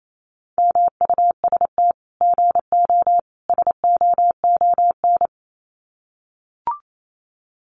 Morse code, heard as a keyed tone, about 28 wpm, 700 Hz. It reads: MUST GO SOON
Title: Common QSO Phrases 28wpm